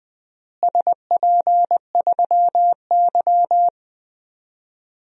Morse code, heard as a keyed tone, about 20 wpm, 700 Hz. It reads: SP3Y
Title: DXCC-100 Call Signs - 4 Characters - Repeated 3x - 20wpm